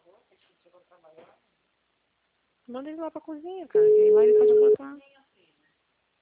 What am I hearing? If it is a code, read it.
Ringback tone (Europe, 425 Hz), with speech.
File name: out- -2025- - - .26637